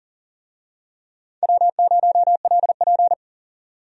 Morse code, heard as a keyed tone, about 40 wpm, 700 Hz. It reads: W0LP
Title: USA Call Signs - 4 Characters - 40wpm